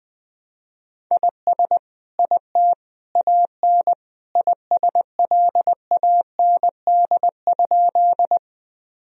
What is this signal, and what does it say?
Morse code, heard as a keyed tone, about 20 wpm, 700 Hz. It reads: IS IT AN ISLAND?
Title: Sentences from Top 1000 Words - Review Part 3 - 20wpm